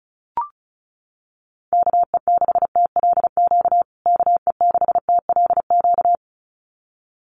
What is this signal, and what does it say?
Morse code, heard as a keyed tone, about 34 wpm, 700 Hz. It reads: KE6TLQ KE6TLQ
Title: USA Call Signs Repeated 2x 35wpm